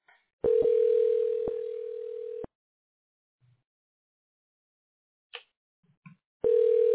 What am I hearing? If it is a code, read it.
Ringback tone (North America), 440 + 480 Hz.